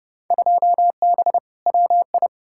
Morse code, heard as a keyed tone, about 30 wpm, 700 Hz. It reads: 2B WI
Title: Field Day Exchanges - 30wpm